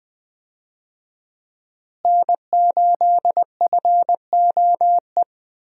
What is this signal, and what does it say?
Morse code, heard as a keyed tone, about 20 wpm, 700 Hz. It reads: N8FOE
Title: USA Call Signs - 5 Characters - Repeated 2x - 20wpm